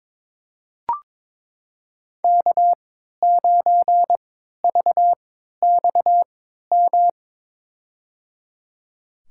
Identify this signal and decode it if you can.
Morse code, heard as a keyed tone, about 22 wpm, spaced as for 12 wpm, 700 Hz. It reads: K9VXM
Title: USA Call Signs - 5 Characters - 3x Character Spacing - 22wpm